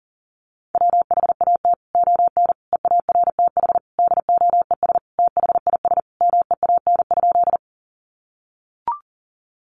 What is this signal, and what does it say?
Morse code, heard as a keyed tone, about 40 wpm, 700 Hz. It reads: WHAT ON EARTH DOES THIS MEAN?
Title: Sentences from Top 200 Words 40wpm